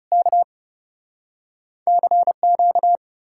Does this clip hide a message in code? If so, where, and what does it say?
Yes: Morse code from 0.1 to 3.0 s, heard as a keyed tone, about 40 wpm, 700 Hz. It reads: K CQ